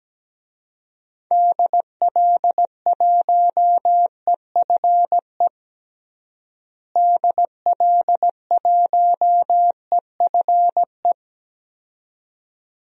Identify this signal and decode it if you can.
Morse code, heard as a keyed tone, about 17 wpm, 700 Hz. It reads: DL1EFE DL1EFE
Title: DXCC-100 Call Signs - 6 Characters - Repeated 2x - 17wpm